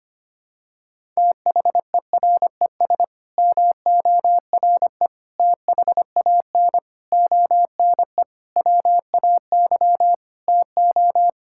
Morse code, heard as a keyed tone, about 25 wpm, 700 Hz. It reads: THERES MORE THAN ONE WAY TO
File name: Sets of 9 Words - 25wpm